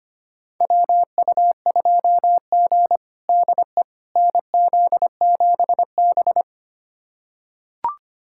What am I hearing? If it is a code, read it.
Morse code, heard as a keyed tone, about 25 wpm, 700 Hz. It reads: WU2G DE NZ7B
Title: USA Callsign DE Callsign - 4 Characters - 25wpm